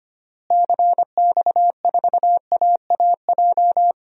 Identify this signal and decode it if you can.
Morse code, heard as a keyed tone, about 25 wpm, 700 Hz. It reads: CX4AAJ
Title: DXCC-100 Call Signs - 6 Characters - Repeated 3x - 25wpm